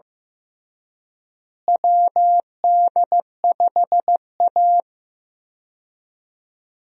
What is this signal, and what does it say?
Morse code, heard as a keyed tone, about 15 wpm, 700 Hz. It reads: WD5A
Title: USA Call Signs - 4 Characters - Repeated 3x - 15wpm